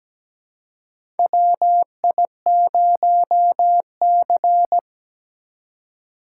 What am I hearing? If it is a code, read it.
Morse code, heard as a keyed tone, about 17 wpm, 700 Hz. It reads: WI0C
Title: USA Call Signs - 4 Characters - 2x Character Spacing - 17wpm